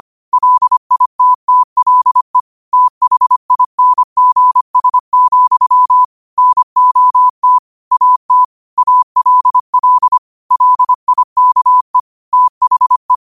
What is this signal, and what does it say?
Morse code, heard as a keyed tone, about 25 wpm, 1000 Hz. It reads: LITTLE THINGS, NOT AT ALL LIKE THE
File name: qotdfiles / 25wpm